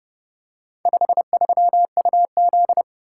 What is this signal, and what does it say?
Morse code, heard as a keyed tone, about 29 wpm, 700 Hz. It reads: 53UZ